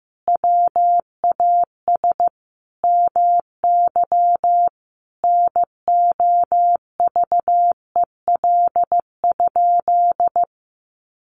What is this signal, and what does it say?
Morse code, heard as a keyed tone, about 15 wpm, 700 Hz. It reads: WAS MY NOVEL?
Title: Sentences from Top 1300 Words 15wpm